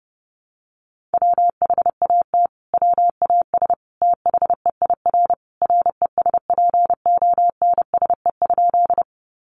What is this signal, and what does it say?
Morse code, heard as a keyed tone, about 28 wpm, 700 Hz. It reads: WHAT WAS THEIR RESPONSE?
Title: Sentences from Top 1400 Words 30wpm